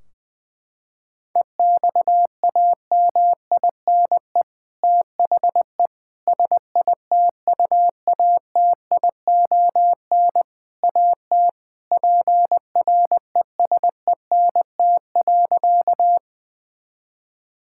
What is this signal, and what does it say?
Morse code, heard as a keyed tone, about 20 wpm, 700 Hz. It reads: EXAMINE THE SITUATION AT PRESENT.
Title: Sentences from Top 1600 Words 20wpm